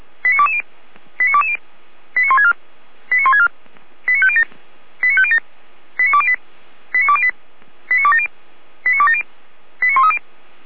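Five-tone selective call (ZVEI), heard as quick runs of five tones, tones of about 70 ms. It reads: 89220 89220 89164 89164 89507 89507 89208 89208 89270 89270 89130
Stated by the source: Received on SDR, ZVEI five-tone call 89080.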